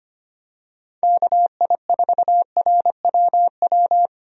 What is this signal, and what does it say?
Morse code, heard as a keyed tone, about 25 wpm, 700 Hz. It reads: KI4RWW
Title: USA Call Signs - 6 Characters - Repeated 2x - 25wpm